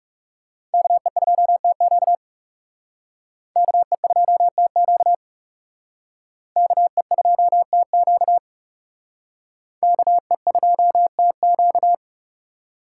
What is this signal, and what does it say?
Morse code, heard as a keyed tone, about 37 wpm, 700 Hz. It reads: KE2TQ KE2TQ KE2TQ KE2TQ
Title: Speed Racing - USA Call Signs 30wpm